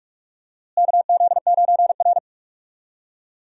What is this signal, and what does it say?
Morse code, heard as a keyed tone, about 42 wpm, 700 Hz. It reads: KZ9R
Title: USA Call Signs - 4 Characters - Repeated 3x - 45wpm